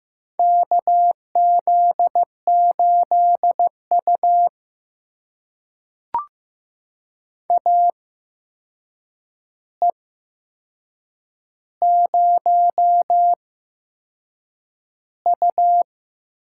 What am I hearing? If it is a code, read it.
Morse code, heard as a keyed tone, about 15 wpm, spaced as for 6 wpm, 700 Hz. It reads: KZ8U AE0U
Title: USA Call Signs - 4 Characters - 8x Character Spacing - 15wpm